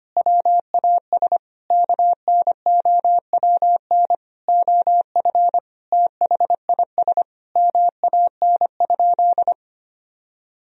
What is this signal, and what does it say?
Morse code, heard as a keyed tone, about 25 wpm, 700 Hz. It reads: WAS KNOWN OF THIS MAN?
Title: Sentences from Top 1000 Words - Review Part 3 - 25wpm